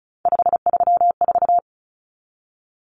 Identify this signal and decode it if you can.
Morse code, heard as a keyed tone, about 34 wpm, 700 Hz. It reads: 534
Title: Common Cut Numbers - 3-Digit 35wpm